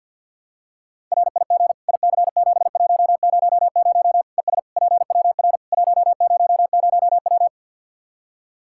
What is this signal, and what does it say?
Morse code, heard as a keyed tone, about 49 wpm, 700 Hz. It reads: RIG IC7100 ES PWR 100W
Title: Practice QSO - Exchange 2 - Info - 50wpm